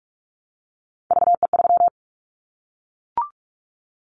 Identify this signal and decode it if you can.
Morse code, heard as a keyed tone, about 44 wpm, 700 Hz. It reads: VE3
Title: Callsign Prefixes for Top 50 DXCC - 45wpm